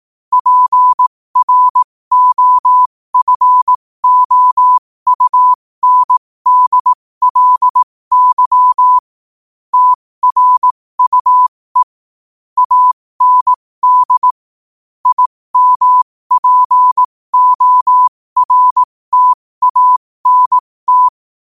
Morse code, heard as a keyed tone, about 18 wpm, spaced as for 15 wpm, 1000 Hz. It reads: PROFOUNDLY TRUE AND IMPORTANT